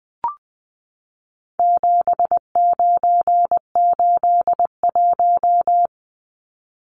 Morse code, heard as a keyed tone, about 20 wpm, 700 Hz. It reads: 7981
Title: Numbers - 4-Digit 20wpm